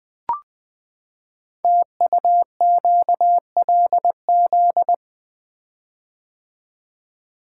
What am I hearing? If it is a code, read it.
Morse code, heard as a keyed tone, about 20 wpm, 700 Hz. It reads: TUQLZ
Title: Code Groups - 5 Letters - 20wpm